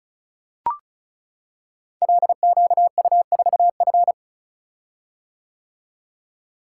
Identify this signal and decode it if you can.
Morse code, heard as a keyed tone, about 36 wpm, 690 Hz. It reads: LQU4F
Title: Code Groups - 5 Letter:Numbers - 35wpm